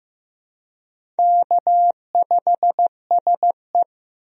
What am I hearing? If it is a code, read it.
Morse code, heard as a keyed tone, about 15 wpm, 700 Hz. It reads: K5SE
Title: USA Call Signs - 4 Characters - Repeated 3x - 15wpm